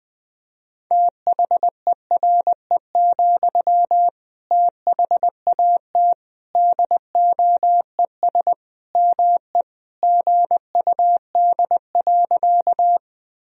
Morse code, heard as a keyed tone, about 20 wpm, 700 Hz. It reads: THERE, THAT DOES ME GUD.
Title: Sentences from Top 200 Words and Abbreviations 20wpm